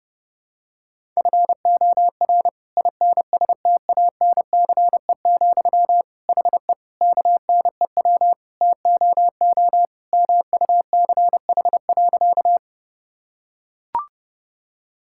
Morse code, heard as a keyed tone, about 32 wpm, 700 Hz. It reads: FOR INSTANCE, HE KNEW TOO MUCH.
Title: Sentences from Top 1000 Words - Review Part 5 - 30wpm